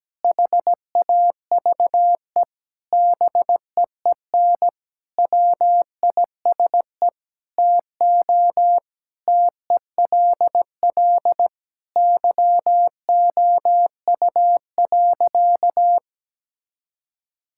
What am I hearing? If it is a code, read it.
Morse code, heard as a keyed tone, about 17 wpm, 700 Hz. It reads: HAVE BEEN WISE TO TELL YOU.
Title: Sentences from Top 1600 Words 17wpm